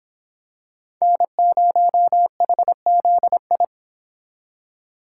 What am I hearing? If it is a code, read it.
Morse code, heard as a keyed tone, about 26 wpm, 700 Hz. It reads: N0HZI